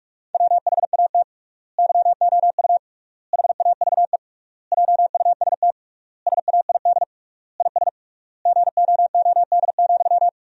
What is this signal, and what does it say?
Morse code, heard as a keyed tone, about 47 wpm, 700 Hz. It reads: WHAT YOU HAVE JUST SAID IS GOOD,
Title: 3x Word-Spacing - Sentences from Top 200 Words - 45wpm